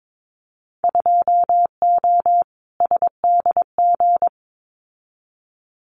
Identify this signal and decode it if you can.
Morse code, heard as a keyed tone, about 22 wpm, 700 Hz. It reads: 2O SDG
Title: Winter Field Day Exchanges 22wpm